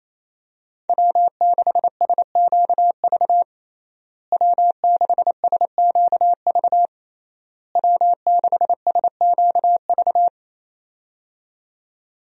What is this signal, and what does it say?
Morse code, heard as a keyed tone, about 28 wpm, 700 Hz. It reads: W6SQV W6SQV W6SQV
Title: USA Call Signs - 5 Characters - Repeated 3x - 28wpm